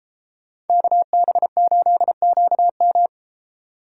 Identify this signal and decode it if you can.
Morse code, heard as a keyed tone, about 33 wpm, 700 Hz. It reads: KB8QM